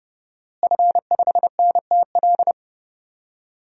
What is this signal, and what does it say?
Morse code, heard as a keyed tone, about 30 wpm, 700 Hz. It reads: F5NTL
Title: DXCC-100 Call Signs - 5 Characters - Repeated 2x - 30wpm